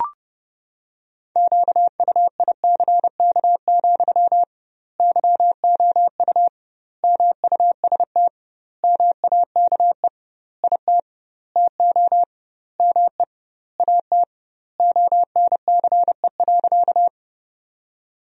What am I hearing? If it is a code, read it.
Morse code, heard as a keyed tone, about 30 wpm, 700 Hz. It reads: QUICK, YOU MUST MAKE IT TO ME AT ONCE.
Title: 2x Word-Spacing - Sentences from Top 500 Words - 30wpm